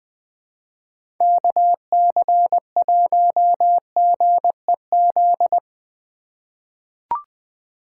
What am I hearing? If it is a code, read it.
Morse code, heard as a keyed tone, about 20 wpm, 700 Hz. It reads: KC1GEZ
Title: USA Call Signs - 2x Character Spacing - 20wpm